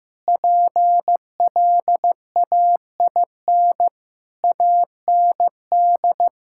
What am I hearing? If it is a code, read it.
Morse code, heard as a keyed tone, about 15 wpm, 700 Hz. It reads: PLAIN AND
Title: Sentences from Top 1000 Words - Review Part 5 - 15wpm